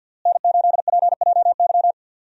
Morse code, heard as a keyed tone, about 52 wpm, 700 Hz. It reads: N8PJY
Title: USA Call Signs - 5 Characters - Repeated 3x - 50wpm